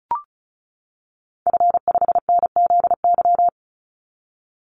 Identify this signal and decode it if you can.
Morse code, heard as a keyed tone, about 34 wpm, 700 Hz. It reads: F5NZY
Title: Call Signs - 35wpm